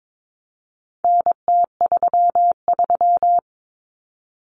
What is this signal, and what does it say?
Morse code, heard as a keyed tone, about 22 wpm, 700 Hz. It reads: NT33
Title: Common Cut Numbers - 4-Digit 22wpm